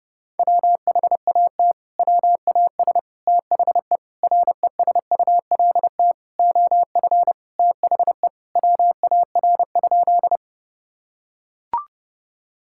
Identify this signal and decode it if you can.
Morse code, heard as a keyed tone, about 30 wpm, 700 Hz. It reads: WHAT WAS THE RESULT OF THE WAR?
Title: Sentences from Top 1000 Words - Review Part 5 - 30wpm